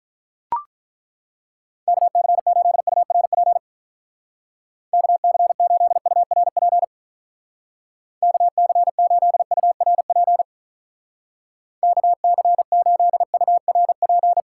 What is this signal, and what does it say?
Morse code, heard as a keyed tone, about 43 wpm, 700 Hz. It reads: KC8URP KC8URP KC8URP KC8URP
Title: Speed Racing - USA 2x3 Call Signs 35wpm